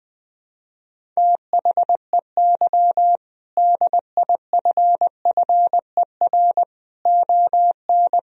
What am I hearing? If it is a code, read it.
Morse code, heard as a keyed tone, about 21 wpm, 700 Hz. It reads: THEY DIFFER ON
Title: Sentences from Top 1000 Words - Review Part 1 - 20wpm